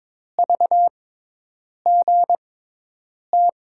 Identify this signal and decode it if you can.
Morse code, heard as a keyed tone, about 22 wpm, spaced as for 8 wpm, 700 Hz. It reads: VGT
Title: USA Call Signs - 6 Characters - 6x Character Spacing - 22wpm